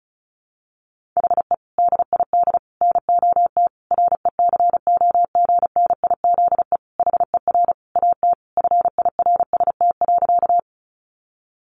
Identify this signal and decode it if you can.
Morse code, heard as a keyed tone, about 34 wpm, 700 Hz. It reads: HE DID NOT RECOGNIZE HER AT FIRST.
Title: Sentences from Top 1400 Words 35wpm